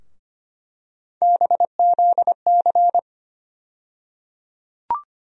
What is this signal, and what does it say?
Morse code, heard as a keyed tone, about 25 wpm, 700 Hz. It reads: BZC
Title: USA Call Sign Postfixes - 3 Letters 25wpm